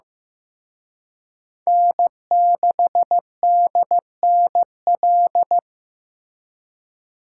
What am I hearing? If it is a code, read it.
Morse code, heard as a keyed tone, about 18 wpm, 700 Hz. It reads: N6DNL
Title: Speed Racing - USA 1x3 Call Signs 15wpm